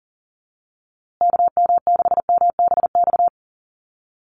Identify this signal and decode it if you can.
Morse code, heard as a keyed tone, about 40 wpm, 700 Hz. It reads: KM6MBX
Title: USA Call Signs Repeated 3x 40wpm